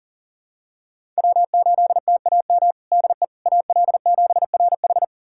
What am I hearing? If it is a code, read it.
Morse code, heard as a keyed tone, about 40 wpm, 700 Hz. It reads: W8TAM DE AL7RH
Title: USA Callsign DE Callsign - 5 Characters - 40wpm